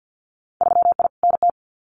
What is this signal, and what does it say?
Morse code, heard as a keyed tone, about 50 wpm, 700 Hz. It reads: 3I NT
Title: Winter Field Day Exchanges Encore 50wpm